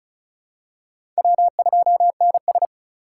Morse code, heard as a keyed tone, about 36 wpm, 700 Hz. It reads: W2NS
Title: USA Call Signs - 4 Characters - 6x Character Spacing - 35wpm